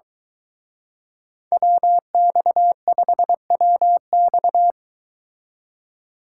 Morse code, heard as a keyed tone, about 23 wpm, 700 Hz. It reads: WX5WX